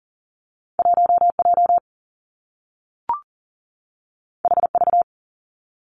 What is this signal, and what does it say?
Morse code, heard as a keyed tone, about 42 wpm, 700 Hz. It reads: 1 J H V
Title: Two Challenging Letter-Number Combinations 40wpm